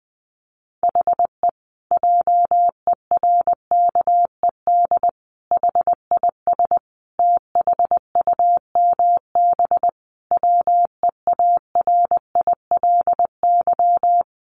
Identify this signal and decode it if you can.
Morse code, heard as a keyed tone, about 20 wpm, 700 Hz. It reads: HE JERKED HIS THUMB WEARILY
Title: A Sound Of Thunder 20wpm